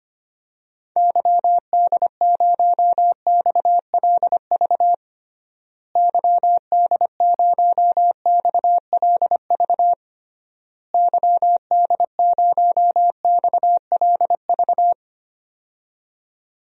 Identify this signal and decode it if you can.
Morse code, heard as a keyed tone, about 25 wpm, 700 Hz. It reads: YD0XLV YD0XLV YD0XLV
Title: DXCC-100 Call Signs - 6 Characters - Repeated 3x - 25wpm